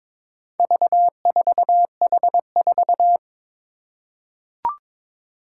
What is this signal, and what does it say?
Morse code, heard as a keyed tone, about 23 wpm, 700 Hz. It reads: V4H4
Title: Four Challenging Letter-Number Combinations 22wpm